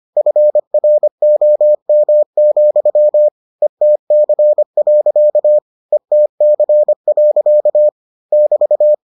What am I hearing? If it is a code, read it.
Morse code, heard as a keyed tone, about 25 wpm, 600 Hz. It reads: FROM, ETC. ETC. <BT>